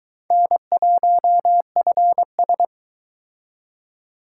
Morse code, heard as a keyed tone, about 23 wpm, 700 Hz. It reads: N1FS